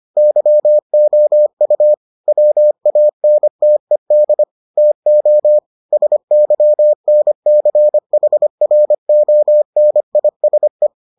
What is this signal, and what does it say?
Morse code, heard as a keyed tone, about 25 wpm, 600 Hz. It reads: YOU WANTED TO SYNCHRONISE